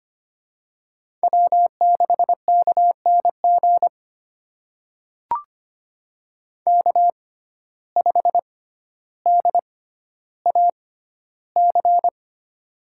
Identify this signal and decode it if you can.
Morse code, heard as a keyed tone, about 25 wpm, spaced as for 13 wpm, 700 Hz. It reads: W6KNG K5DAC
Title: USA Call Signs - 6x Character Spacing - 25wpm